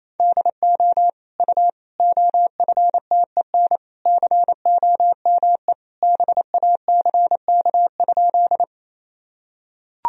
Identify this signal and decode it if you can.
Morse code, heard as a keyed tone, about 28 wpm, 700 Hz. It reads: DO U OFTEN COME BACK?
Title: Sentences from Top 300 Words and Abbreviations 28wpm